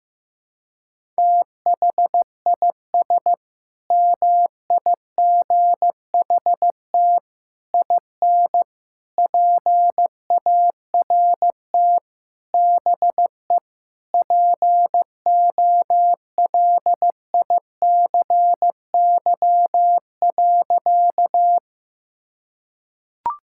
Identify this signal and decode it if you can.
Morse code, heard as a keyed tone, about 15 wpm, 700 Hz. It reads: THIS MIGHT IN PART BE POLICY.
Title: Sentences from Top 1000 Words - Review Part 5 - 15wpm